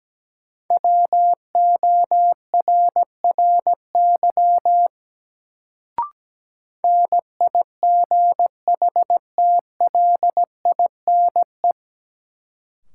Morse code, heard as a keyed tone, about 17 wpm, 700 Hz. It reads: WORRY NIGHTLINE
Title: Sets of 5 Words - New Word Drill - 17wpm